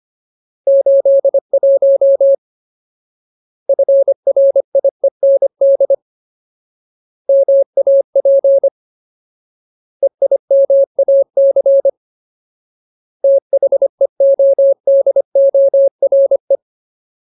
Morse code, heard as a keyed tone, about 25 wpm, 550 Hz. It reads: 81 FRIEND MAP EIMAC THEODORE
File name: Ultimate Word List / 25WPM Ultimate